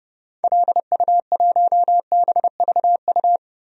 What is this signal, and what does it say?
Morse code, heard as a keyed tone, about 30 wpm, 700 Hz. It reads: LU1BVU
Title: DXCC-100 Call Signs - 6 Characters - Repeated 3x - 30wpm